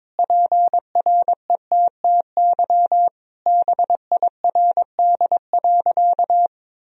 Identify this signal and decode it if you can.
Morse code, heard as a keyed tone, about 21 wpm, 700 Hz. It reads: PRETTY BIRD.
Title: Sentences from Top 1000 Words - Review Part 2 - 22wpm